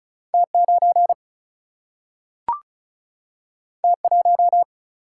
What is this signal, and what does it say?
Morse code, heard as a keyed tone, about 36 wpm, 700 Hz. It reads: T9 T1
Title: Single-Letter Single-Number 35wpm